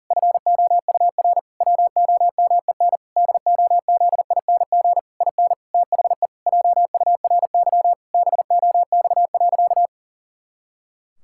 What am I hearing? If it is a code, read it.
Morse code, heard as a keyed tone, about 40 wpm, 700 Hz. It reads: FOUR WOMEN DOZING IN THE JURY BOX.
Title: Pangram Challenge v2 - 40wpm